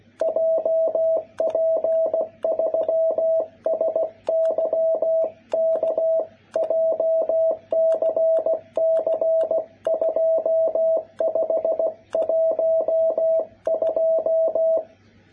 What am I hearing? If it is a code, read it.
Morse code, heard as a keyed tone, about 16 wpm, 660 Hz. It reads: JP3SYK JCC2512
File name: jh8cbh / 2021cw